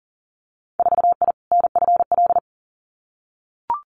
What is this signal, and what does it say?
Morse code, heard as a keyed tone, about 42 wpm, 700 Hz. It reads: 4I NFL
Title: Winter Field Day Exchanges Encore 40wpm